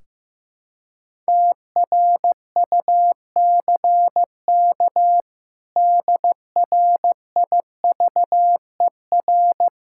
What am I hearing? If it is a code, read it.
Morse code, heard as a keyed tone, about 15 wpm, 700 Hz. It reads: TRUCK DRIVER
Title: K1USN SST QSO Practice 15wpm